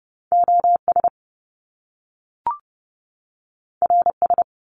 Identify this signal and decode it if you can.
Morse code, heard as a keyed tone, about 30 wpm, 700 Hz. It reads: OS RS
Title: Top Two Letter Combinations in Words 30wpm